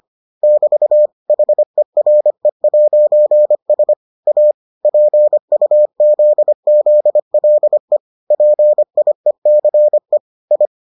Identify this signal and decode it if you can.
Morse code, heard as a keyed tone, about 25 wpm, 600 Hz. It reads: <BT> HERE'S A PUZZLE PIECE I